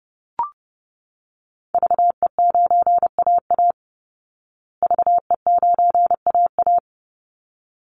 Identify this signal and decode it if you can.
Morse code, heard as a keyed tone, about 28 wpm, 700 Hz. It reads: VE9AA VE9AA
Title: Call Signs Repeated 3x - 30wpm